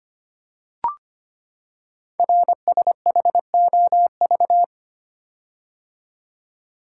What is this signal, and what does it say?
Morse code, heard as a keyed tone, about 25 wpm, 700 Hz. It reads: RSHOV
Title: Code Groups - 5 Letter:Numbers - 25wpm